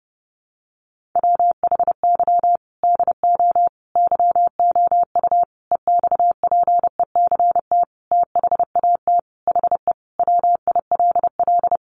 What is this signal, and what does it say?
Morse code, heard as a keyed tone, about 28 wpm, 700 Hz. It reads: WHY DO YOU EXPECT THAT HE WILL
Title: Sentences from Top 1100 Words 30wpm